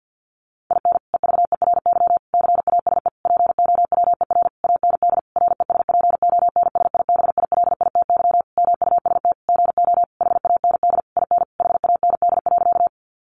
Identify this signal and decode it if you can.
Morse code, heard as a keyed tone, about 48 wpm, 700 Hz. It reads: IN EVERY CASE POWER AND RESPONSIBILITY MUST GO HAND IN HAND.
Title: Sentences from Top 1200 Words 50wpm